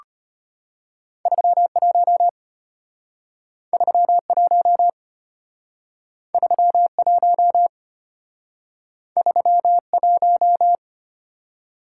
Morse code, heard as a keyed tone, about 32 wpm, 700 Hz. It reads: 31 31 31 31